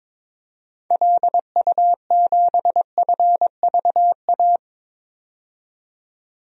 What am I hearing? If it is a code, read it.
Morse code, heard as a keyed tone, about 22 wpm, 700 Hz. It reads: LU7FVA
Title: DXCC-100 Call Signs - 6 Characters - 22wpm